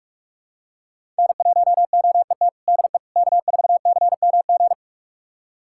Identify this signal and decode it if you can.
Morse code, heard as a keyed tone, about 44 wpm, 700 Hz. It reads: N1OET DE K4CMG
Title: USA Callsign DE Callsign - 5 Characters - 45wpm